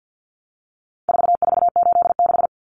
Morse code, heard as a keyed tone, about 48 wpm, 700 Hz. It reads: 4486
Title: Numbers - 4-Digit 50wpm